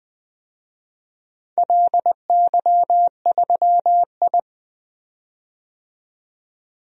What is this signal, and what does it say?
Morse code, heard as a keyed tone, about 20 wpm, 700 Hz. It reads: LY3I
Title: DXCC-100 Call Signs - 4 Characters - 20wpm